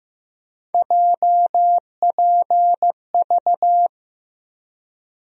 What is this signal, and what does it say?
Morse code, heard as a keyed tone, about 15 wpm, 700 Hz. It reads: JPV
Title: USA Call Sign Postfixes - 3 Letters 15wpm